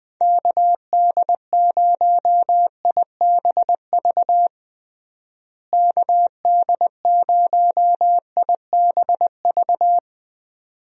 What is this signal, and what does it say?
Morse code, heard as a keyed tone, about 20 wpm, 700 Hz. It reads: KD0IBV KD0IBV
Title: USA Call Signs - 6 Characters - Repeated 2x - 20wpm